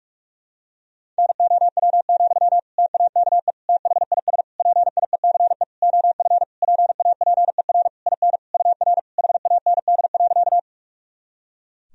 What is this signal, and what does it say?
Morse code, heard as a keyed tone, about 46 wpm, 700 Hz. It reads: NOW, TAKE THIS PIECE OF PAPER IN UR HAND.
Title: Sentences from Top 400 Words and Abbreviations 45wpm